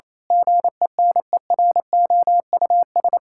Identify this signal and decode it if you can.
Morse code, heard as a keyed tone, about 28 wpm, 700 Hz. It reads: GENEROUS
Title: Pangram Challenge v2 - 28wpm